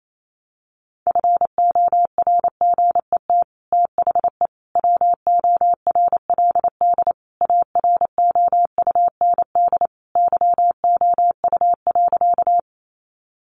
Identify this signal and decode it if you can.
Morse code, heard as a keyed tone, about 28 wpm, 700 Hz. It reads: FORGET THE WORLD AROUND YOU.
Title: Sentences from Top 1200 Words 28wpm